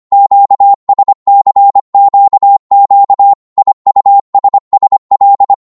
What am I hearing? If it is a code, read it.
Morse code, heard as a keyed tone, about 25 wpm, 800 Hz. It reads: QSCQQ IUSSL
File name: Morse / ltr-25wpm